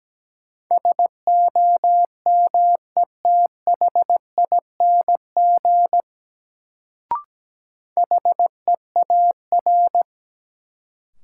Morse code, heard as a keyed tone, about 17 wpm, 700 Hz. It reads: SOMETHING HEAR
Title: Sets of 5 Words - New Word Drill - 17wpm